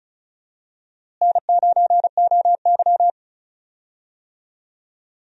Morse code, heard as a keyed tone, about 34 wpm, 700 Hz. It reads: N9OY
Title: USA Call Signs - 4 Characters - Repeated 2x - 35wpm